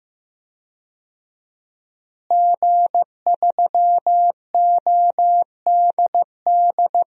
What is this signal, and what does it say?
Morse code, heard as a keyed tone, about 15 wpm, 700 Hz. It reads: G3ODD
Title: DXCC-100 Call Signs - 5 Characters - Repeated 3x - 15wpm